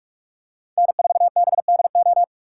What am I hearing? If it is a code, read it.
Morse code, heard as a keyed tone, about 46 wpm, 700 Hz. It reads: N4BDO
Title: USA Call Signs - 5 Characters - 3x Character Spacing - 45wpm